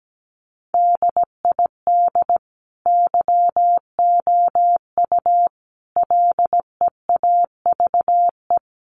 Morse code, heard as a keyed tone, about 17 wpm, 700 Hz. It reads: DID YOU LEAVE
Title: Sentences from Top 1000 Words 17wpm